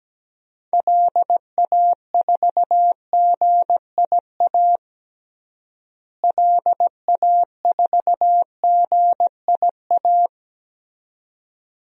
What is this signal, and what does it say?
Morse code, heard as a keyed tone, about 17 wpm, 700 Hz. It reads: LA4GIA LA4GIA
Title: DXCC-100 Call Signs - 6 Characters - Repeated 2x - 17wpm